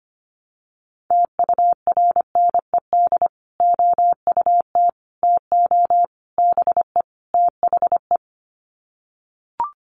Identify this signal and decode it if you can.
Morse code, heard as a keyed tone, about 25 wpm, 700 Hz. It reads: TURNED OUT TO BE THE
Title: Sets of 5 Words Encore 25wpm